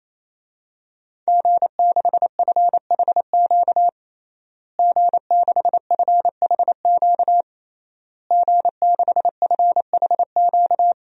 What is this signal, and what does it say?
Morse code, heard as a keyed tone, about 28 wpm, 700 Hz. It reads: G6FHQ G6FHQ G6FHQ
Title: DXCC-100 Call Signs - 5 Characters - Repeated 3x - 28wpm